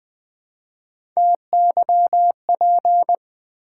Morse code, heard as a keyed tone, about 20 wpm, 700 Hz. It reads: TYP
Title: Build A Word - 1 Letter At A Time - 4-Letter Words - 20wpm